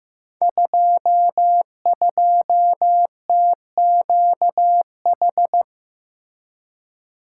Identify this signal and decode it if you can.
Morse code, heard as a keyed tone, about 15 wpm, 700 Hz. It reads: 22TQH
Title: Code Groups - 5 Letter:Numbers - 15wpm